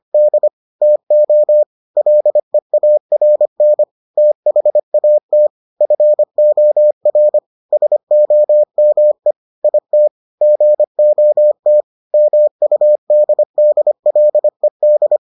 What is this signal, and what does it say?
Morse code, heard as a keyed tone, about 25 wpm, 600 Hz. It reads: D TO LEARN THAT FOR SOME IT GOT MUDDLED